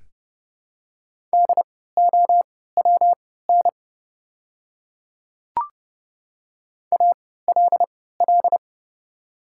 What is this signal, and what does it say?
Morse code, heard as a keyed tone, about 30 wpm, spaced as for 17 wpm, 700 Hz. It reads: DOWN ALL
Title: 3x Character Spacing - Top 100 Words - 30wpm